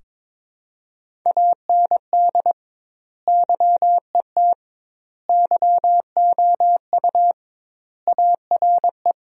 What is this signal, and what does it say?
Morse code, heard as a keyed tone, about 22 wpm, 700 Hz. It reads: AND YET YOU ARE
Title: 2x Word-Spacing - Sentences from Top 500 Words - 22wpm